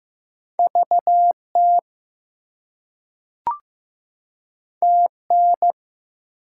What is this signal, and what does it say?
Morse code, heard as a keyed tone, about 15 wpm, 700 Hz. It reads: VT TN
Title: US States and Canadian Provinces 15wpm